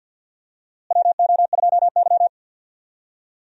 Morse code, heard as a keyed tone, about 52 wpm, 700 Hz. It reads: WO2Y
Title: USA Call Signs - 4 Characters - Repeated 3x - 50wpm